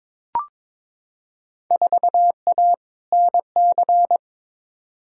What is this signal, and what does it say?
Morse code, heard as a keyed tone, about 22 wpm, 700 Hz. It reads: 4A NC
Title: Field Day Exchanges - 22wpm